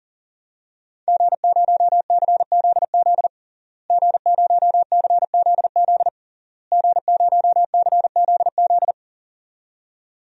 Morse code, heard as a keyed tone, about 40 wpm, 700 Hz. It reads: G0CZZ G0CZZ G0CZZ
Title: DXCC-100 Call Signs - 5 Characters - Repeated 3x - 40wpm